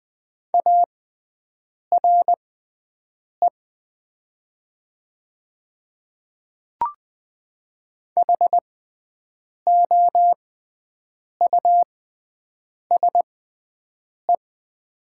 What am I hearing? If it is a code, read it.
Morse code, heard as a keyed tone, about 20 wpm, spaced as for 7 wpm, 700 Hz. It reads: ARE HOUSE
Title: 6x Character Spacing - Top 300 Words - 20wpm